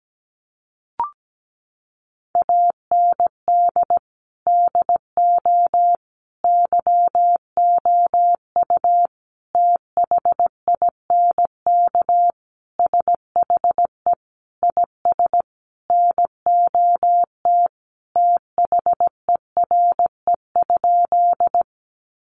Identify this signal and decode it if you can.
Morse code, heard as a keyed tone, about 17 wpm, 700 Hz. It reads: AND DO YOU THINK SHE IS NOT THERE?
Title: Sentences from Top 600 Words 17wpm